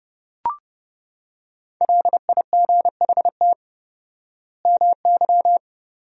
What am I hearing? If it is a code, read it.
Morse code, heard as a keyed tone, about 32 wpm, 700 Hz. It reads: LIGHT MY
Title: 4x Word-Spacing - Sentences from Top 400 Words - 30wpm